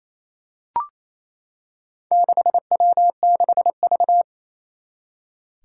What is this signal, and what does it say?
Morse code, heard as a keyed tone, about 28 wpm, 700 Hz. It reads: 6W6V
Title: Four Challenging Letter-Number Combinations 28wpm